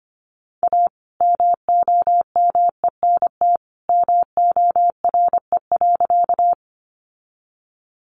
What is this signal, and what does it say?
Morse code, heard as a keyed tone, about 25 wpm, 700 Hz. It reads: A MOMENT MORE.
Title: Sentences from Top 1000 Words 25wpm